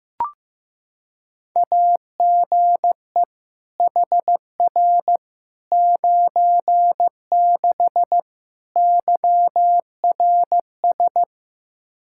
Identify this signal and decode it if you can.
Morse code, heard as a keyed tone, about 15 wpm, 700 Hz. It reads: AGE HR 96 YRS
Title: K1USN SST QSO Practice 15wpm